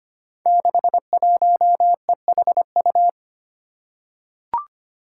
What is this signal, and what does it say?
Morse code, heard as a keyed tone, about 25 wpm, 700 Hz. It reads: 61EHU
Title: Five Challenging Letter-Number Combinations 25wpm